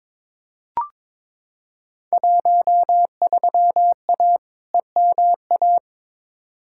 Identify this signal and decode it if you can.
Morse code, heard as a keyed tone, about 22 wpm, 700 Hz. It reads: 13A EMA
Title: Field Day Exchanges Encore - 22wpm